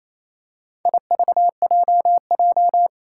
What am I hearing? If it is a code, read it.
Morse code, heard as a keyed tone, about 28 wpm, 700 Hz. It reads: IVJJ
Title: Four Challenging Letter-Number Combinations 28wpm